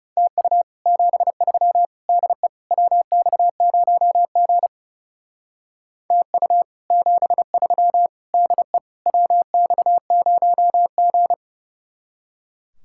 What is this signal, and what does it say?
Morse code, heard as a keyed tone, about 39 wpm, 700 Hz. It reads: TU 73 DE WX0G TU 73 DE WX0G